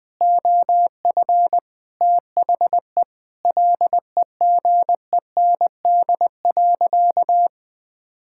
Morse code, heard as a keyed tone, about 20 wpm, 700 Hz. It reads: OF THE LEGEND.
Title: Sentences with 2-Word Phrases - Spoken Beginning - 20wpm